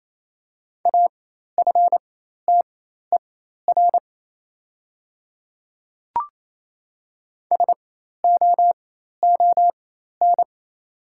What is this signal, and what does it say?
Morse code, heard as a keyed tone, about 28 wpm, spaced as for 13 wpm, 700 Hz. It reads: AFTER SOON
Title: 4x Character Spacing - Top 500 Words - 28wpm